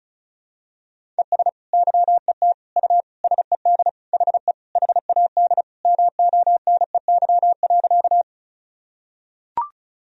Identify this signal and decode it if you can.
Morse code, heard as a keyed tone, about 34 wpm, 700 Hz. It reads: ES YET U SED HE HAD MONEY.
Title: Sentences from Top 500 Words and Abbreviations 35wpm